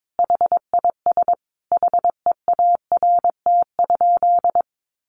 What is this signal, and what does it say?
Morse code, heard as a keyed tone, about 22 wpm, 700 Hz. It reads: HIS HEART?
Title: Sentences from Top 1200 Words 22wpm